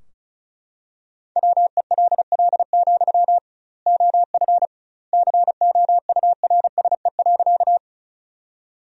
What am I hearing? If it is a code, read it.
Morse code, heard as a keyed tone, about 34 wpm, 700 Hz. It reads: WELL, OF COURSE.
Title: 2x Word-Spacing - Sentences from Top 500 Words - 35wpm